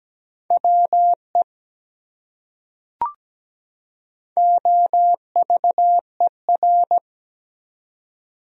Morse code, heard as a keyed tone, about 17 wpm, 700 Hz. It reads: WE OVER
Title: Top 100 Words 17wpm